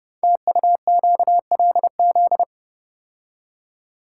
Morse code, heard as a keyed tone, about 28 wpm, 700 Hz. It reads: TUQLZ
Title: Code Groups - 5 Letters - 30wpm